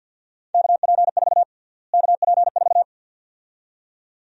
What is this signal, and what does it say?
Morse code, heard as a keyed tone, about 53 wpm, 700 Hz. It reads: KP4 KP4
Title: USA Call Sign Prefixes - Repeated 2x - 50wpm